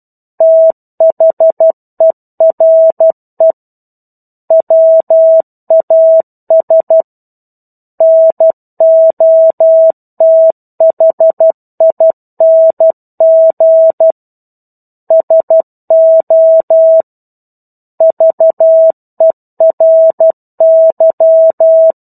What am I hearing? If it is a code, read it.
Morse code, heard as a keyed tone, about 12 wpm, 650 Hz. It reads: THERE WAS NOTHING SO VERY
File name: 01-Chapter-01 12 WPM